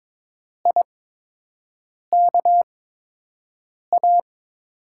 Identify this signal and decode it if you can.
Morse code, heard as a keyed tone, about 21 wpm, spaced as for 6 wpm, 700 Hz. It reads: IKA
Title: USA Call Signs - 5 Characters - 8x Character Spacing - 22wpm